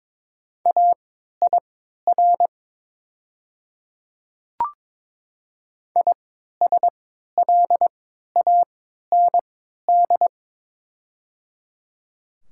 Morse code, heard as a keyed tone, about 22 wpm, spaced as for 12 wpm, 690 Hz. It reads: AIR ISLAND
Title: 3x Character Spacing - Top 500 Words - 22wpm